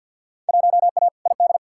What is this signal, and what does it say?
Morse code, heard as a keyed tone, about 52 wpm, 700 Hz. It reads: 1A ID
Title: Field Day Exchanges - 50wpm